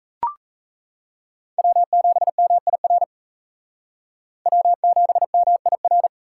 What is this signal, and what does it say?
Morse code, heard as a keyed tone, about 37 wpm, 700 Hz. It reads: W7MIR W7MIR